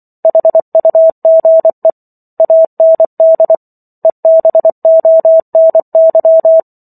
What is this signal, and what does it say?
Morse code, heard as a keyed tone, about 24 wpm, 650 Hz. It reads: HUGE AND EBONY
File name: Chapter 15 - 24 WPM